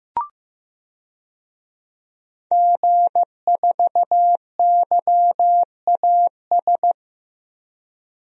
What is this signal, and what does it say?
Morse code, heard as a keyed tone, about 15 wpm, 700 Hz. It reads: G4YAS
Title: DXCC-100 Call Signs - 5 Characters - Repeated 2x - 15wpm